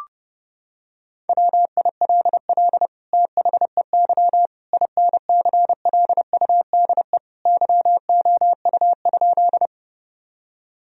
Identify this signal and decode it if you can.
Morse code, heard as a keyed tone, about 30 wpm, 700 Hz. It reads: WILL THEY INCLUDE YOU?
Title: Sentences from Top 1000 Words - Review Part 3 - 30wpm